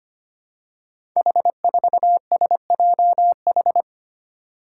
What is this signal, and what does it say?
Morse code, heard as a keyed tone, about 25 wpm, 700 Hz. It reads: H4SJH
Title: Five Challenging Letter-Number Combinations 25wpm